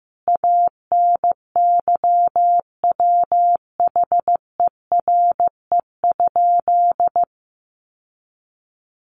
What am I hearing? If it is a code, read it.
Morse code, heard as a keyed tone, about 15 wpm, 700 Hz. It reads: ANYWHERE?
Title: Sentences from Top 1400 Words 15wpm